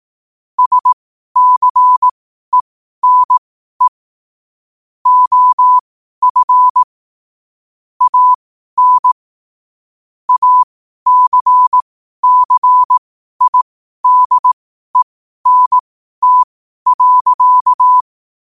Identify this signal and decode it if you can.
Morse code, heard as a keyed tone, about 18 wpm, spaced as for 12 wpm, 1000 Hz. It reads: SCENE OF AN ACCIDENT.
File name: qotdfiles / 12wpm